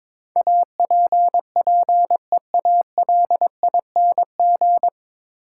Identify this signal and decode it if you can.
Morse code, heard as a keyed tone, about 22 wpm, 700 Hz. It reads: APPEALING
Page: Morse Code Ninja practice 22wpm